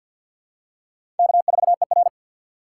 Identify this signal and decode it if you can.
Morse code, heard as a keyed tone, about 48 wpm, 700 Hz. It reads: K4ER
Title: USA Call Signs - 4 Characters - 6x Character Spacing - 50wpm